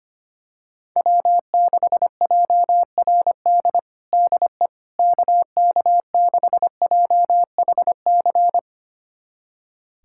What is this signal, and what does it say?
Morse code, heard as a keyed tone, about 25 wpm, 700 Hz. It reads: W6JRD DE KK6JHC
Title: USA Callsign DE Callsign 25wpm